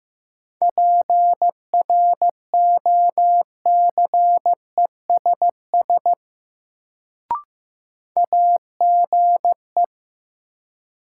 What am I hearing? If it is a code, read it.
Morse code, heard as a keyed tone, about 15 wpm, 700 Hz. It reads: PROCESS AGE
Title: Sets of 3 Words - New Word Drill - 15wpm